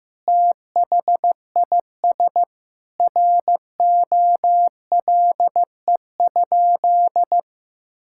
Morse code, heard as a keyed tone, about 15 wpm, 700 Hz. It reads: THIS ROLE?
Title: Sentences from Top 1000 Words - Review Part 6 - 15wpm